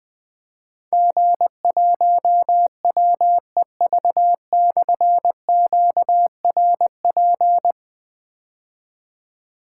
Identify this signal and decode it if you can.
Morse code, heard as a keyed tone, about 20 wpm, 700 Hz. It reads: G1WEV/QRP
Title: Difficult Call Signs 20wpm